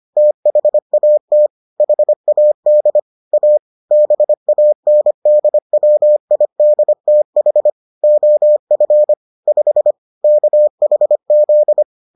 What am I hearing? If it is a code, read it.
Morse code, heard as a keyed tone, about 25 wpm, 600 Hz. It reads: THAT HAD A BANDWIDTH OF 5 KHZ